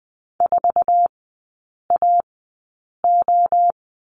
Morse code, heard as a keyed tone, about 20 wpm, spaced as for 8 wpm, 700 Hz. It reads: 4AO
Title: 047 Lesson 9e - Number 4 - Three Characters 20wpm